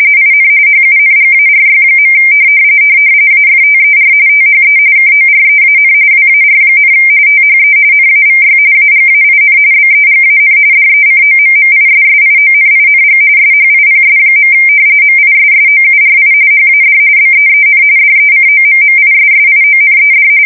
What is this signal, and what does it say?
RTTY, about 45.45 baud. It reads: IN HUNGARY. AFTER THE KOREAN WAR, THE NAVY AGAIN BEGAN DECOMMISSIONING SHIPS. MISSOURI HAD BEEN DECOMMISSIONED
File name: fsk170 / battleshipnj